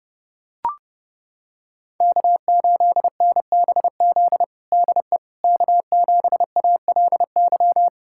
Morse code, heard as a keyed tone, about 31 wpm, 700 Hz. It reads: K8NBZ DE K7ALY
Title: USA Callsign DE Callsign - 5 Characters - 30wpm